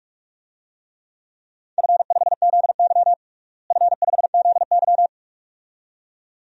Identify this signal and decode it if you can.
Morse code, heard as a keyed tone, about 44 wpm, 700 Hz. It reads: F5ZY F5ZY
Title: DXCC-100 Call Signs - 4 Characters - Repeated 2x - 45wpm